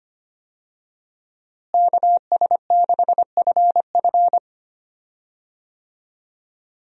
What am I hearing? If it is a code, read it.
Morse code, heard as a keyed tone, about 25 wpm, 700 Hz. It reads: KS6FF
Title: USA Call Signs - 5 Characters - Repeated 2x - 25wpm